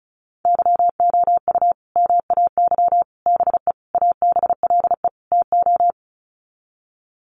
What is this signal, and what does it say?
Morse code, heard as a keyed tone, about 34 wpm, 700 Hz. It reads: YOU MAY BE ABLE TO
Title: Sets of 5 Words Encore 35wpm